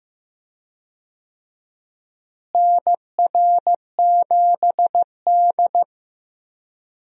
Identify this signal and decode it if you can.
Morse code, heard as a keyed tone, about 15 wpm, 700 Hz. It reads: NR7D
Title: USA Call Signs - 4 Characters - Repeated 3x - 15wpm